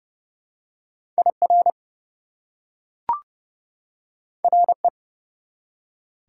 Morse code, heard as a keyed tone, about 30 wpm, 700 Hz. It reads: IR RE
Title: Prefixes and Suffixes 30wpm